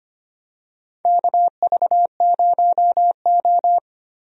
Morse code, heard as a keyed tone, about 25 wpm, 700 Hz. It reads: KV0O
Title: USA Call Signs - 4 Characters - Repeated 2x - 25wpm